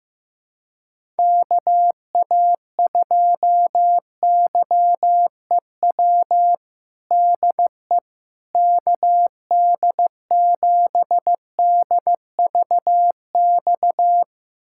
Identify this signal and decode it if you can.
Morse code, heard as a keyed tone, about 15 wpm, 700 Hz. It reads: KA2YEW DE KD7DVX
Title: USA Callsign DE Callsign - 6 Characters - 15wpm